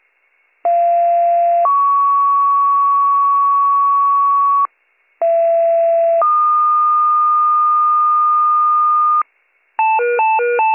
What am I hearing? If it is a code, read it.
Two-tone page, heard as a one-second tone, then a three-second tone, about 690 then 1120 Hz, about 670 then 1220 Hz. It is heard from 0.6 to 9.2 s.